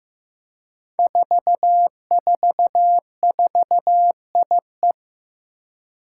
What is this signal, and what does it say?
Morse code, heard as a keyed tone, about 15 wpm, 700 Hz. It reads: 444IE
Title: Five Challenging Letter-Number Combinations 15wpm